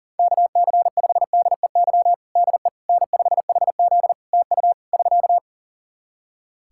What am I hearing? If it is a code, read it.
Morse code, heard as a keyed tone, about 40 wpm, 700 Hz. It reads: KC5DEY DE N5HZ TU <SK>
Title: Practice QSO - Exchange 4 - Ending - 40wpm